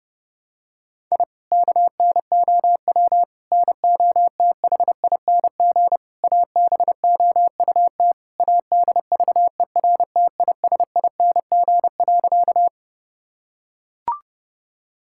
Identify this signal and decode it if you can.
Morse code, heard as a keyed tone, about 30 wpm, 700 Hz. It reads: I KNOW NOTHING ABOUT ADVERTISING.
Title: Sentences from Top 1000 Words - Review Part 6 - 30wpm